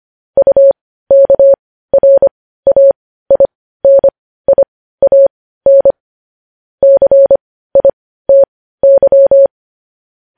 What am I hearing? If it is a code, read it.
Morse code, heard as a keyed tone, about 25 wpm, spaced as for 15 wpm, 550 Hz. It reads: UKRAINIAN CITY
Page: news-headlines-in-morse-code-at-15-wpm